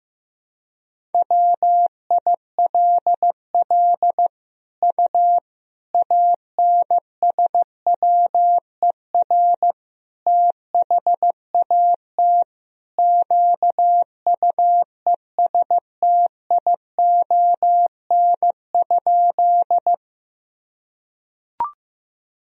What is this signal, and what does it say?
Morse code, heard as a keyed tone, about 16 wpm, 700 Hz. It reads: WILL U ANSWER THAT QUESTION?
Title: Sentences from Top 400 Words and Abbreviations 15wpm